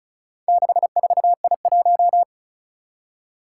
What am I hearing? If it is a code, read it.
Morse code, heard as a keyed tone, about 36 wpm, 700 Hz. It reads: 64I1
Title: Four Challenging Letter-Number Combinations 35wpm